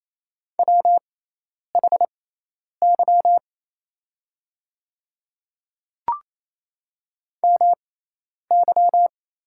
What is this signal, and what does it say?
Morse code, heard as a keyed tone, about 28 wpm, spaced as for 10 wpm, 700 Hz. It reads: WHY MY
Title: 6x Character Spacing - Top 200 Words - 28wpm